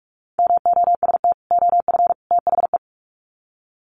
Morse code, heard as a keyed tone, about 44 wpm, 700 Hz. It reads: MOST OF THE
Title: Sets of 3 Words 45wpm